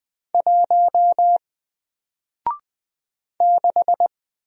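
Morse code, heard as a keyed tone, about 20 wpm, 700 Hz. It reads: 1 6
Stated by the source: Single letter and number Morse drill at 20 wpm, from N4LCD.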